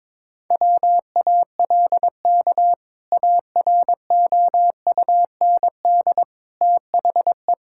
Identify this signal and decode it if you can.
Morse code, heard as a keyed tone, about 22 wpm, 700 Hz. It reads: WALK AROUND THE
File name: practice / 22wpm / Sentences with 2-Word Phrases - 22wpm